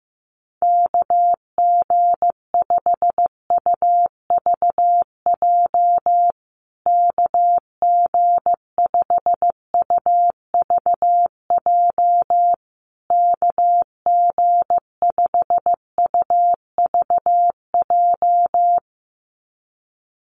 Morse code, heard as a keyed tone, about 15 wpm, 700 Hz. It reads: KG5UVJ KG5UVJ KG5UVJ
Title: USA Call Signs Repeated 3x 15wpm